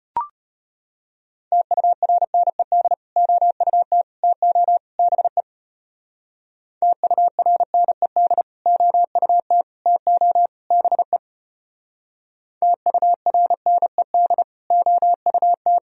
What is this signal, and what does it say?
Morse code, heard as a keyed tone, about 32 wpm, 700 Hz. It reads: TURNED OUT TO BE TURNED OUT TO BE TURNED OUT